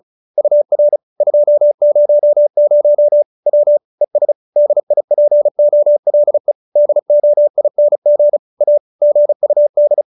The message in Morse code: UR 200 W ES DIPOLE DOING A GUD

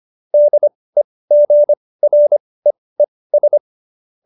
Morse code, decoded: DEGREES

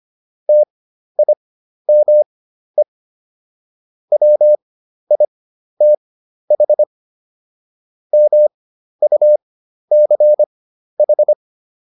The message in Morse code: TIME WITH MUCH

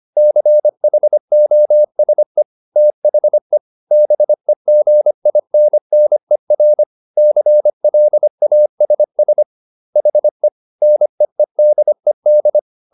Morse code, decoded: CHOSE THE BEGINNER CLASS HE NEEDED

